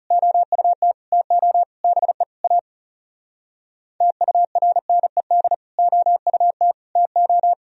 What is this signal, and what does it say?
Morse code, heard as a keyed tone, about 38 wpm, 700 Hz. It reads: OUT TO BE A TURNED OUT TO